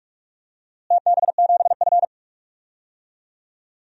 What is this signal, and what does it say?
Morse code, heard as a keyed tone, about 46 wpm, 700 Hz. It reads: TB7F
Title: DXCC-100 Call Signs - 4 Characters - Repeated 3x - 45wpm